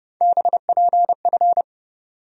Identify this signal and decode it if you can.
Morse code, heard as a keyed tone, about 30 wpm, 700 Hz. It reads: BPF
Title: USA Call Sign Postfixes - 3 Letters 30wpm